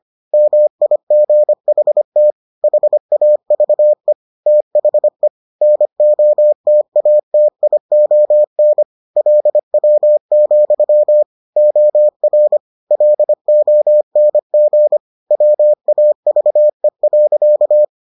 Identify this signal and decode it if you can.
Morse code, heard as a keyed tone, about 25 wpm, 600 Hz. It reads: MIGHT HAVE THE NOTATION LW, OR LONG WAVE.